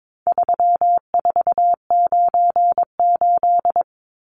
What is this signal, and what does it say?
Morse code, heard as a keyed tone, about 22 wpm, 700 Hz. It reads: 3498
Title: Numbers - 4-Digit 22wpm